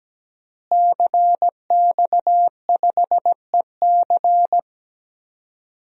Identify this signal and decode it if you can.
Morse code, heard as a keyed tone, about 17 wpm, 700 Hz. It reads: CX5EC
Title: DXCC-100 Call Signs - 5 Characters - Repeated 3x - 17wpm